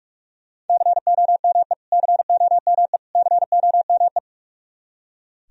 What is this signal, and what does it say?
Morse code, heard as a keyed tone, about 44 wpm, 700 Hz. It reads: COME COME COME
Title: Top 100 Words Repeated 45wpm